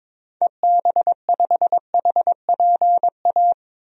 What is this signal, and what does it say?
Morse code, heard as a keyed tone, about 22 wpm, 700 Hz. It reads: EB5HPA
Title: DXCC-100 Call Signs - 6 Characters - Repeated 3x - 22wpm